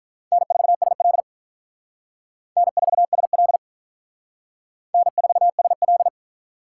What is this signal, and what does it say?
Morse code, heard as a keyed tone, about 42 wpm, 700 Hz. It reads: N4SL N4SL N4SL
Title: Speed Racing - USA 1x2 Call Signs 35wpm